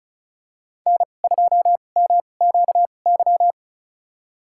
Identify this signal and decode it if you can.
Morse code, heard as a keyed tone, about 36 wpm, spaced as for 25 wpm, 700 Hz. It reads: N2MQY
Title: USA Call Signs - 5 Characters - 2x Character Spacing - 35wpm